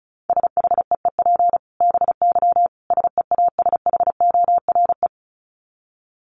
Morse code, heard as a keyed tone, about 34 wpm, 700 Hz. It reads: SHEEP BY SEASHORE
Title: Dit Challenge 35wpm